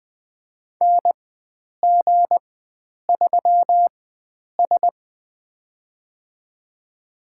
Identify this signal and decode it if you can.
Morse code, heard as a keyed tone, about 20 wpm, spaced as for 9 wpm, 700 Hz. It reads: NG3S
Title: USA Call Signs - 4 Characters - 4x Character Spacing - 20wpm